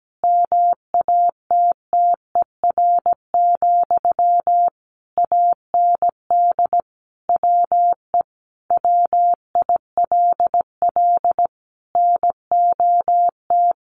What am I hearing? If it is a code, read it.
Morse code, heard as a keyed tone, about 17 wpm, 700 Hz. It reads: MATTER, AND WE WILL NOT
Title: Sentences from Top 1100 Words 17wpm